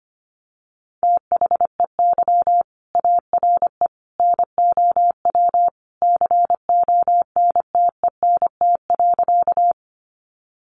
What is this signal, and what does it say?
Morse code, heard as a keyed tone, about 25 wpm, 700 Hz. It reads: THEY ARE NOW CONTENT.
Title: Sentences from Top 900 Words 25wpm